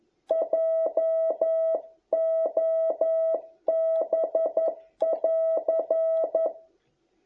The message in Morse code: JOB <AR>

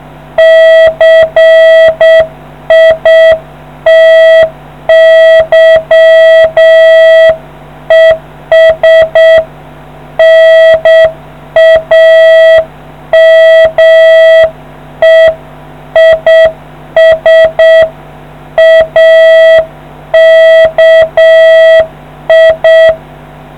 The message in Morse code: CITY ES NAME IS AKI

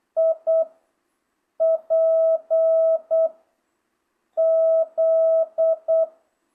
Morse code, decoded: I P Z

